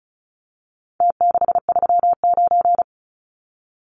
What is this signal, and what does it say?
Morse code, heard as a keyed tone, about 34 wpm, 700 Hz. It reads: T639
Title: Numbers - Contest Style - 4-Digit 35wpm